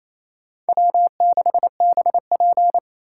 Morse code, heard as a keyed tone, about 28 wpm, 700 Hz. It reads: W6BP
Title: USA Call Signs - 4 Characters - 4x Character Spacing - 28wpm